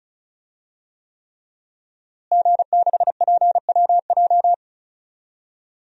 Morse code, heard as a keyed tone, about 36 wpm, 700 Hz. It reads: G6PWJ
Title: DXCC-100 Call Signs - 5 Characters - Repeated 3x - 35wpm